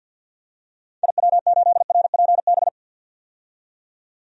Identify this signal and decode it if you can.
Morse code, heard as a keyed tone, about 52 wpm, 700 Hz. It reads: IW8RPB
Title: DXCC-100 Call Signs - 6 Characters - 50wpm